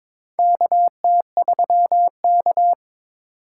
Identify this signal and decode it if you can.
Morse code, heard as a keyed tone, about 22 wpm, 700 Hz. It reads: KT3K
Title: USA Call Signs - 4 Characters - 6x Character Spacing - 22wpm